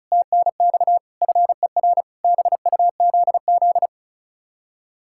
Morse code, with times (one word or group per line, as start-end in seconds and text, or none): 0.1-1.0 TNX
1.2-2.0 FER
2.2-3.9 BUZZ